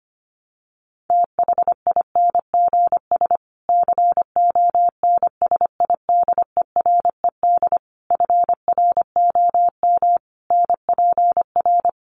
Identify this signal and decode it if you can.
Morse code, heard as a keyed tone, about 25 wpm, 700 Hz. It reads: THINGS CONSIDERED FROM NPR
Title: Sets of 5 Words Encore 25wpm